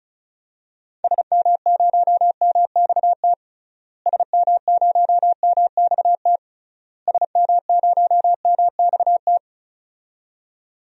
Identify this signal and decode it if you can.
Morse code, heard as a keyed tone, about 36 wpm, 700 Hz. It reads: SM0MXT SM0MXT SM0MXT
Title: DXCC-100 Call Signs - 6 Characters - Repeated 3x - 35wpm